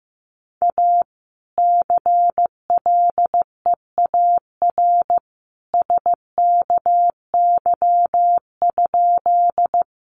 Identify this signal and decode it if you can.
Morse code, heard as a keyed tone, about 15 wpm, 700 Hz. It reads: A CLEAR SKY?
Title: Sentences from Top 1200 Words 15wpm